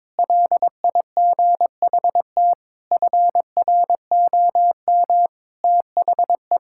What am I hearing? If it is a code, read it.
Morse code, heard as a keyed tone, about 22 wpm, 700 Hz. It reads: LIGHT FROM THE
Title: Sentences from Top 1000 Words - Review Part 5 - 22wpm